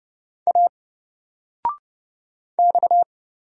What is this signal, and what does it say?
Morse code, heard as a keyed tone, about 29 wpm, 700 Hz. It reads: A X
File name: Single-Letter-Number-30wpm